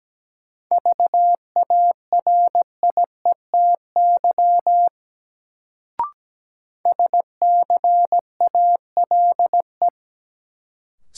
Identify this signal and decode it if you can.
Morse code, heard as a keyed tone, about 17 wpm, 700 Hz. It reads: VARIETY SCALE
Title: Sentences from Top 1000 Words - Review Part 4 - New Word Drill - 17wpm